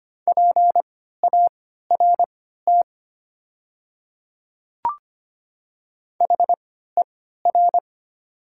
Morse code, heard as a keyed tone, about 25 wpm, spaced as for 14 wpm, 700 Hz. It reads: PART HER
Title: 3x Character Spacing - Top 200 Words - 25wpm